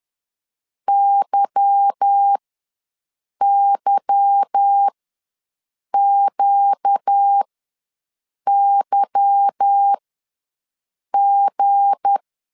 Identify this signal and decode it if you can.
Morse code, heard as a keyed tone, about 10 wpm, spaced as for 6 wpm, 800 Hz. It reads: YYQYG